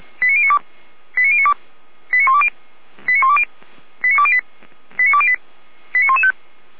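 Five-tone selective call (ZVEI), heard as quick runs of five tones, tones of about 80 ms. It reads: 89092 89092 89120 89120 89208 89208 89115
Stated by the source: Received on an SDR; ZVEI five-tone code 89094.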